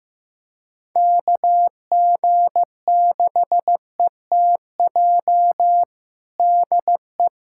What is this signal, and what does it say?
Morse code, heard as a keyed tone, about 15 wpm, 700 Hz. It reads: KG6ETJ DE
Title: USA Callsign DE Callsign 15wpm